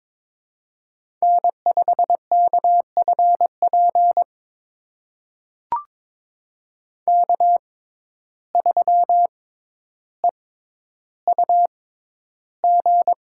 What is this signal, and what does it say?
Morse code, heard as a keyed tone, about 22 wpm, spaced as for 11 wpm, 700 Hz. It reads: N5KFP K3EUG
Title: USA Call Signs - 5 Characters - 6x Character Spacing - 22wpm